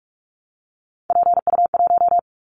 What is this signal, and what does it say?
Morse code, heard as a keyed tone, about 44 wpm, 700 Hz. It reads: PU1
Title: Callsign Prefixes for Top 100 DXCC - 45wpm